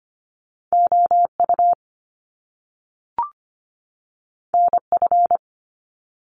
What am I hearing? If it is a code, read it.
Morse code, heard as a keyed tone, about 25 wpm, 700 Hz. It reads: OU NF
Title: Top Two Letter Combinations in Words 25wpm